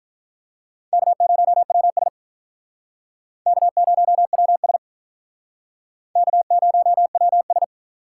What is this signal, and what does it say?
Morse code, heard as a keyed tone, about 46 wpm, 700 Hz. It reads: K0WS K0WS K0WS